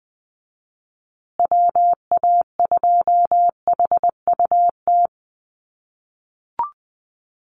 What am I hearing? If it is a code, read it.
Morse code, heard as a keyed tone, about 20 wpm, 700 Hz. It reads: WA2HUT
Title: USA Call Signs Repeated 2x 20wpm